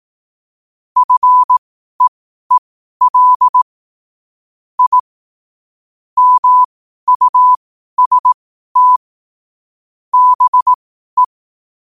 Morse code, heard as a keyed tone, about 18 wpm, spaced as for 12 wpm, 1000 Hz. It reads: FEEL I MUST BE
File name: qotdfiles / 12wpm